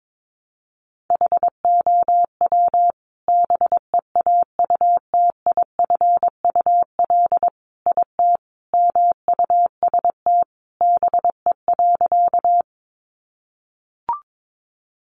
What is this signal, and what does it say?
Morse code, heard as a keyed tone, about 22 wpm, 700 Hz. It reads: HOW BEAUTIFUL IT MUST BE.
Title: Sentences from Top 900 Words 22wpm